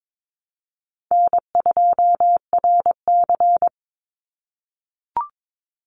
Morse code, heard as a keyed tone, about 22 wpm, 700 Hz. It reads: N2RC
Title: Easy Call Signs - 22wpm